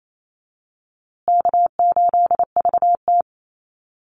Morse code, heard as a keyed tone, about 28 wpm, 700 Hz. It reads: K8VT
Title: USA Call Signs Repeated 2x 28wpm